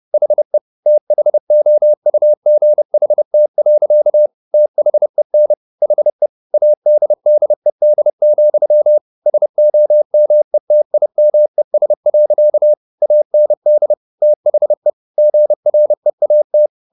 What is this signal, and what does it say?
Morse code, heard as a keyed tone, about 30 wpm, 600 Hz. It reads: HE THOUGHT. THEN HE ADDED, SOMETIMES. AND THE GREAT